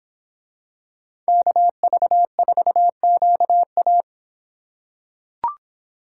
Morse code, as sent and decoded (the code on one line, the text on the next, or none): -.- ...- ....- --.- .-
KV4QA